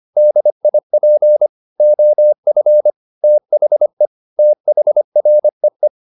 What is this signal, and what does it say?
Morse code, heard as a keyed tone, about 25 wpm, 600 Hz. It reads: DIP OF THE THREE
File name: the old man and the sea part 1 25wpm